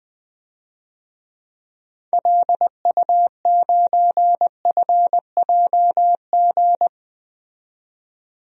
Morse code, heard as a keyed tone, about 20 wpm, 700 Hz. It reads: LU9FJG
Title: DXCC-100 Call Signs - 6 Characters - 20wpm